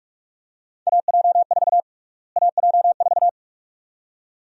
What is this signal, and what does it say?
Morse code, heard as a keyed tone, about 41 wpm, 700 Hz. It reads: AJ4 AJ4
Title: USA Call Sign Prefixes - Repeated 2x - 45wpm